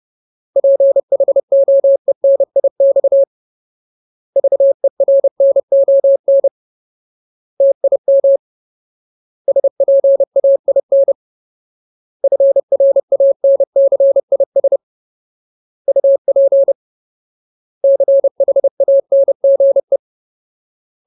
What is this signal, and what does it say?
Morse code, heard as a keyed tone, about 30 wpm, 550 Hz. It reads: PHOENIX VERNON TIM SPAIN FRANCIS UP CHANGE
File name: Ultimate Word List / 30WPM Ultimate